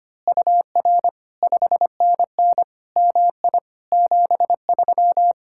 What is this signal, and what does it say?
Morse code, heard as a keyed tone, about 24 wpm, 700 Hz. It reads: UR 5NN MI 73
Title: Speed-Racing Common QSO Phrases Encore - 25wpm